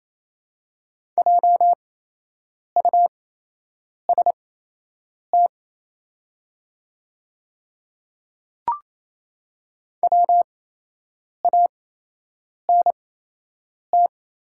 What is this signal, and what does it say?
Morse code, heard as a keyed tone, about 28 wpm, spaced as for 8 wpm, 700 Hz. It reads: JUST WANT
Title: 8x Character Spacing - Top 200 Words - 28wpm